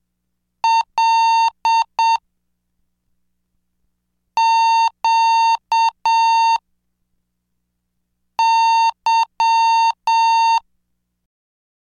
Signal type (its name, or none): Morse code